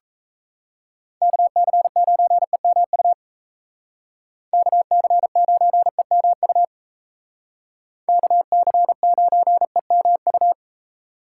Morse code, heard as a keyed tone, about 35 wpm, 700 Hz. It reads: KC9EMU KC9EMU KC9EMU
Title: Speed Racing - USA 2x3 Call Signs 28wpm